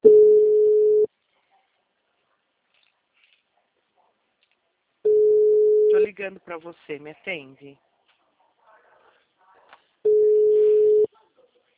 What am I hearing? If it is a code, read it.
Ringback tone (Europe, 425 Hz), with speech.
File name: out- -2006- - - .22364